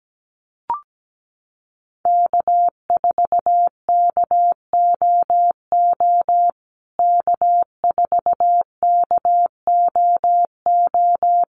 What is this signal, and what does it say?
Morse code, heard as a keyed tone, about 17 wpm, 700 Hz. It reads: K4KOO K4KOO
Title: USA Call Signs Repeated 2x 17wpm